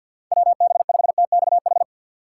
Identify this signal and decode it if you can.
Morse code, heard as a keyed tone, about 52 wpm, 700 Hz. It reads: WB5TXH
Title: USA Call Signs - 6 Characters - 4x Character Spacing - 50wpm